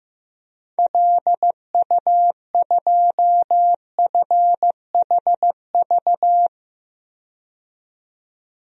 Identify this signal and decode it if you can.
Morse code, heard as a keyed tone, about 15 wpm, 700 Hz. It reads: LU2FHV
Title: DXCC-100 Call Signs - 6 Characters - 15wpm